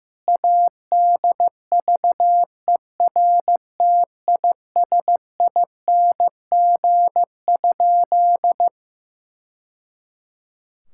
Morse code, heard as a keyed tone, about 15 wpm, 700 Hz. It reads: ADVERTISING?
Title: Sentences from Top 1000 Words - Review Part 6 - 15wpm